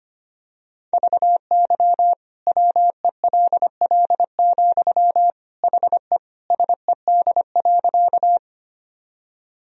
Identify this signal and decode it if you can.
Morse code, heard as a keyed tone, about 25 wpm, 700 Hz. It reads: VY WELL, HE SED.
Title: Sentences from Top 200 Words and Abbreviations 25wpm